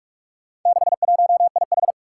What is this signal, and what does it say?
Morse code, heard as a keyed tone, about 46 wpm, 700 Hz. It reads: 61IH
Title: Four Challenging Letter-Number Combinations 45wpm